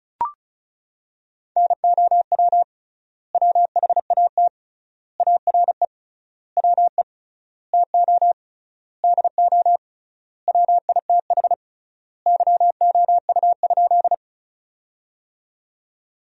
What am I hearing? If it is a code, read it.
Morse code, heard as a keyed tone, about 36 wpm, 700 Hz. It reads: NOW WHAT ARE WE TO DO WITH YOU?
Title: 3x Word-Spacing - Sentences from Top 100 Words - 35wpm